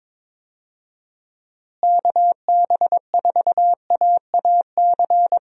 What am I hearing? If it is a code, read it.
Morse code, heard as a keyed tone, about 22 wpm, 700 Hz. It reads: KB4AAC
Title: USA Call Signs - 6 Characters - Repeated 2x - 22wpm